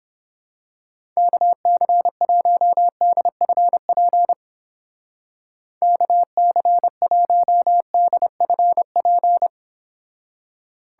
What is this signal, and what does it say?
Morse code, heard as a keyed tone, about 28 wpm, 700 Hz. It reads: KC1DFP KC1DFP